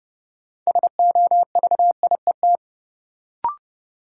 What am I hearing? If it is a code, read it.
Morse code, heard as a keyed tone, about 31 wpm, 700 Hz. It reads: SOVIET
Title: Sets of 5 Words Encore - New Word Drill - 30wpm